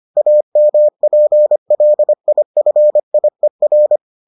Morse code, heard as a keyed tone, about 25 wpm, 600 Hz. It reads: AMPLIFIER